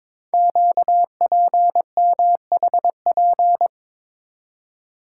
Morse code, heard as a keyed tone, about 23 wpm, 700 Hz. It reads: QPMHP
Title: Code Groups - 5 Letters - 22wpm